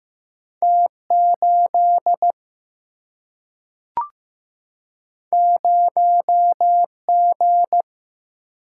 Morse code, heard as a keyed tone, about 15 wpm, 700 Hz. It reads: T 8 0 G
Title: Single-Letter Single-Number 15wpm